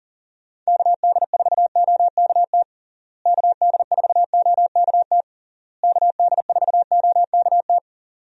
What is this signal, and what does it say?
Morse code, heard as a keyed tone, about 40 wpm, 700 Hz. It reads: KD4OKT KD4OKT KD4OKT
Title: USA Call Signs - 6 Characters - Repeated 3x - 40wpm